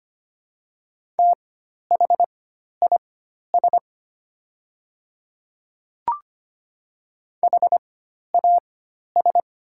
Morse code, heard as a keyed tone, about 25 wpm, spaced as for 12 wpm, 700 Hz. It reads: THIS HAS
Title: 4x Character Spacing - Top 100 Words - 25wpm